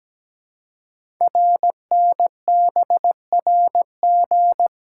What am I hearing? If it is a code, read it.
Morse code, heard as a keyed tone, about 17 wpm, 700 Hz. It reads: RNBRG
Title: Code Groups - 5 Letters - 17wpm